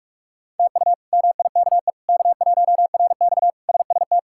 Morse code, heard as a keyed tone, about 41 wpm, 700 Hz. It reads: TU MIKE K1RX SST
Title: K1USN SST QSO Practice - 45wpm